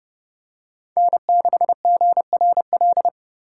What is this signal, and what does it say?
Morse code, heard as a keyed tone, about 30 wpm, 700 Hz. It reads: N6GRL
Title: USA Call Signs - 5 Characters - 2x Character Spacing - 30wpm